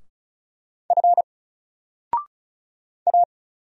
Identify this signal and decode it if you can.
Morse code, heard as a keyed tone, about 34 wpm, 700 Hz. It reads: F A
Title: Single Letter-Number 35wpm